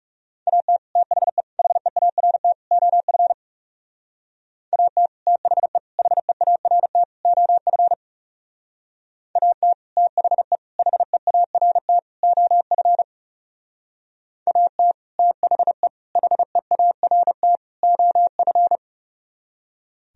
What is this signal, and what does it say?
Morse code, heard as a keyed tone, about 39 wpm, 700 Hz. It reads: AT THE HEART OF AT THE HEART OF AT THE HEART OF AT THE HEART OF